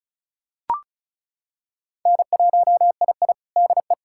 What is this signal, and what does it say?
Morse code, heard as a keyed tone, about 36 wpm, 700 Hz. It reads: N1II DE
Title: USA Callsign DE Callsign - 4 Characters - 35wpm